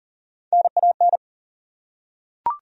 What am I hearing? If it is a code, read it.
Morse code, heard as a keyed tone, about 37 wpm, 700 Hz. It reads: NAN
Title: USA Common Names - Max 3 Letters - 40wpm